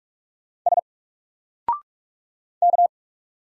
Morse code, heard as a keyed tone, about 44 wpm, 700 Hz. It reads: S K
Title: Single Character (Letter:Number:Punctuation:Prosign) 45wpm